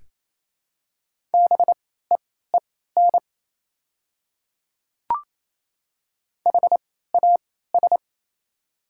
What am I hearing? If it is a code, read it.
Morse code, heard as a keyed tone, about 28 wpm, spaced as for 16 wpm, 700 Hz. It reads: BEEN HAS
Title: 3x Character Spacing - Top 100 Words - 28wpm